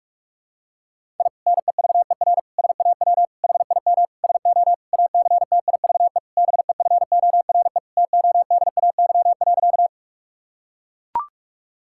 Morse code, heard as a keyed tone, about 41 wpm, 700 Hz. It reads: I NEVER SAW HIM SO ACTIVE BEFORE TODAY.
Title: Sentences from Top 1000 Words - Review Part 6 - 45wpm